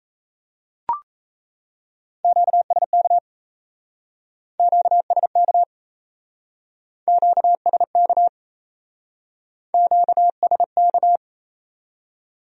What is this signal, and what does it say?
Morse code, heard as a keyed tone, about 37 wpm, 700 Hz. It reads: QSK QSK QSK QSK